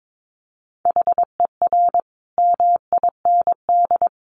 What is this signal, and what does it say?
Morse code, heard as a keyed tone, about 22 wpm, 700 Hz. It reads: HER MIND
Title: Sentences from Top 1500 Words 22wpm